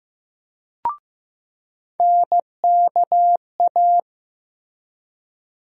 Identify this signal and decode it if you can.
Morse code, heard as a keyed tone, about 15 wpm, 700 Hz. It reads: NKA
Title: USA Call Sign Postfixes - 3 Letters 15wpm